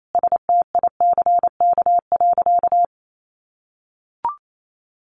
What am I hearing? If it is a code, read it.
Morse code, heard as a keyed tone, about 28 wpm, 700 Hz. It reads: STICK.
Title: Sentences from Top 1200 Words 28wpm